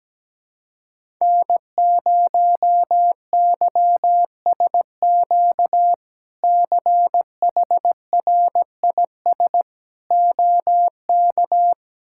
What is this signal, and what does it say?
Morse code, heard as a keyed tone, about 17 wpm, 700 Hz. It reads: N0YSQ CHRIS OK
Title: K1USN SST QSO Practice - 17wpm